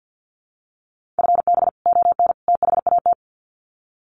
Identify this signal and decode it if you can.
Morse code, heard as a keyed tone, about 50 wpm, 700 Hz. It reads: FB ON THAT
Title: Common QSO Phrases 50wpm